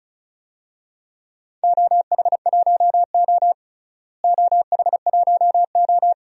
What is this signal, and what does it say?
Morse code, heard as a keyed tone, about 36 wpm, 690 Hz. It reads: OH1O OH1O
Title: DXCC-100 Call Signs - 4 Characters - Repeated 3x - 35wpm